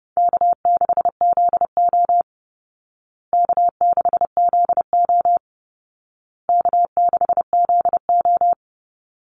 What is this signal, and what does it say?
Morse code, heard as a keyed tone, about 30 wpm, 700 Hz. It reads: K6ZO K6ZO K6ZO
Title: Easy Call Signs Repeated 3x - 30wpm